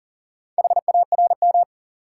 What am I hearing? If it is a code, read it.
Morse code, heard as a keyed tone, about 40 wpm, 700 Hz. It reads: HARM
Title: Four Letter Words - 40wpm